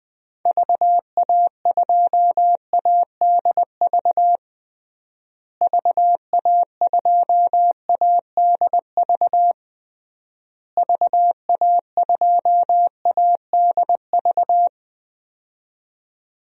Morse code, heard as a keyed tone, about 20 wpm, 700 Hz. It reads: VA2ADV VA2ADV VA2ADV
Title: DXCC-100 Call Signs - 6 Characters - Repeated 3x - 20wpm